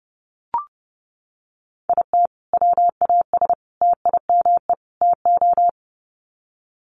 Morse code, heard as a keyed tone, about 30 wpm, 700 Hz. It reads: IT WAS TIME TO
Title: Sets of 4 Words 30wpm